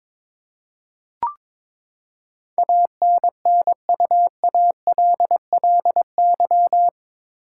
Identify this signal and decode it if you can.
Morse code, heard as a keyed tone, about 22 wpm, 700 Hz. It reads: ANNUALLY
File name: practice / 22wpm / Eight Letter Words - 22wpm